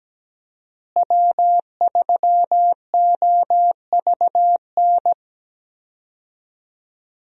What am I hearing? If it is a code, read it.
Morse code, heard as a keyed tone, about 17 wpm, 700 Hz. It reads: W3OVN
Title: USA Call Signs - 5 Characters - Repeated 3x - 17wpm